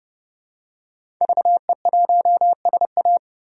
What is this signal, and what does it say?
Morse code, heard as a keyed tone, about 30 wpm, 700 Hz. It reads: VE1SA
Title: DXCC-100 Call Signs - 5 Characters - 30wpm